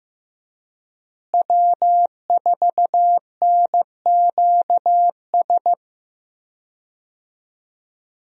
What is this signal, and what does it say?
Morse code, heard as a keyed tone, about 15 wpm, 700 Hz. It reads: W4NQS
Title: USA Call Signs - 5 Characters - Repeated 2x - 15wpm